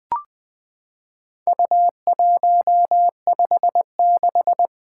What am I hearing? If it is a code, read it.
Morse code, heard as a keyed tone, about 20 wpm, 700 Hz. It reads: U156
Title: Four Letter-Number Combinations - 20wpm